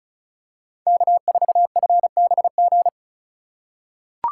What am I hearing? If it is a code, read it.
Morse code, heard as a keyed tone, about 36 wpm, 700 Hz. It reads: K4FBG
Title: USA Call Signs - 5 Characters - 3x Character Spacing - 35wpm